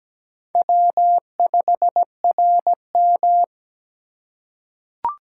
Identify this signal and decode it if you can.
Morse code, heard as a keyed tone, about 17 wpm, 700 Hz. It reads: W5RM
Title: USA Call Signs - 4 Characters - 3x Character Spacing - 17wpm